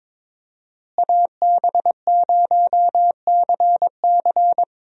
Morse code, heard as a keyed tone, about 22 wpm, 700 Hz. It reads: AB0CC
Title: USA Call Signs - 5 Characters - Repeated 2x - 22wpm